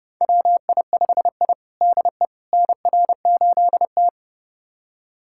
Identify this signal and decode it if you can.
Morse code, heard as a keyed tone, about 30 wpm, 700 Hz. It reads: WI5I DE NR8T
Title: USA Callsign DE Callsign - 4 Characters - 30wpm